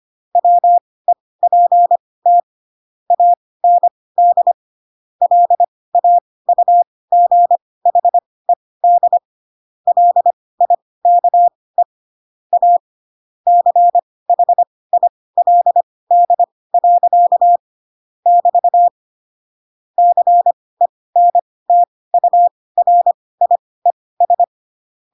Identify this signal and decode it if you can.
Morse code, heard as a keyed tone, about 25 wpm, spaced as for 18 wpm, 700 Hz. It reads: WEPT AND LAUGHED LIKE A CHILD. <BT> CENTURIES
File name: Pinocchio / Pinocchio ch1 12 25 ik1hge